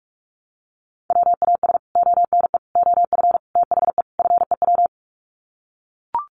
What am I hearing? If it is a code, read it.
Morse code, heard as a keyed tone, about 44 wpm, 700 Hz. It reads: WAS ONE OF THE FEW
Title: Sets of 5 Words Encore 45wpm